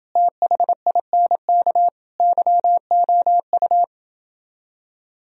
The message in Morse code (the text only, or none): THINK YOU